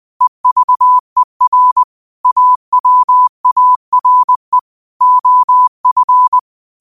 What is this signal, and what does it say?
Morse code, heard as a keyed tone, about 20 wpm, 1000 Hz. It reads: EVER AWARE OF